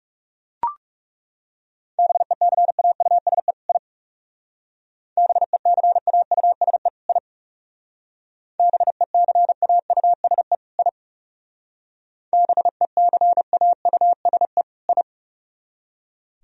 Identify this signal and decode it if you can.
Morse code, heard as a keyed tone, about 37 wpm, 700 Hz. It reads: BECAUSE I BECAUSE I BECAUSE I BECAUSE I
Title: Speed-Racing - Sets of 2 Words 30wpm